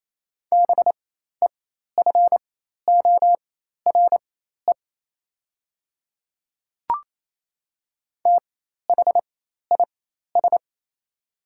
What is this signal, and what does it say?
Morse code, heard as a keyed tone, about 28 wpm, spaced as for 13 wpm, 700 Hz. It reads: BEFORE THIS
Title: 4x Character Spacing - Top 500 Words - 28wpm